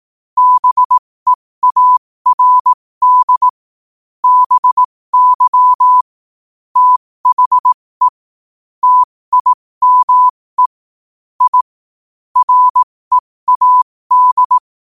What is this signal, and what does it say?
Morse code, heard as a keyed tone, about 18 wpm, spaced as for 15 wpm, 1000 Hz. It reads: BEARD BY THE TIME I READ
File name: qotdfiles / 15wpm